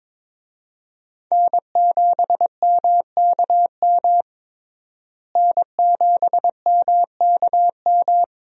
Morse code, heard as a keyed tone, about 22 wpm, 700 Hz. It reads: N7MKM N7MKM
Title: USA Call Signs - 5 Characters - Repeated 2x - 22wpm